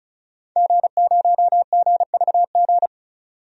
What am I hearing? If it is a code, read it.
Morse code, heard as a keyed tone, about 36 wpm, 700 Hz. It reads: G0GVG
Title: DXCC-100 Call Signs - 5 Characters - 35wpm